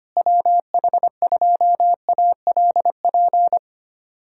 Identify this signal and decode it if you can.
Morse code, heard as a keyed tone, about 25 wpm, 700 Hz. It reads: WH2ALP
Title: USA Callsign DE Callsign - 6 Characters - 25wpm